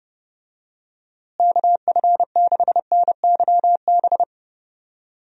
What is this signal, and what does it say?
Morse code, heard as a keyed tone, about 30 wpm, 700 Hz. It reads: KF6NYB
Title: USA Call Signs - 6 Characters - Repeated 3x - 30wpm